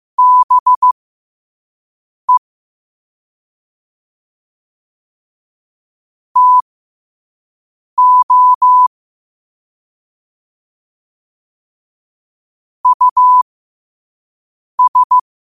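Morse code, heard as a keyed tone, about 15 wpm, spaced as for 5 wpm, 1000 Hz. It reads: BE TO US
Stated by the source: Morse code practice at 5 wpm.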